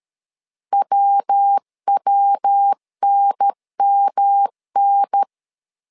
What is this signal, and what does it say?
Morse code, heard as a keyed tone, about 13 wpm, 800 Hz. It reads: WWNMN
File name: LEZIONI / 3 MWRO / 3 MW 12wpm 1633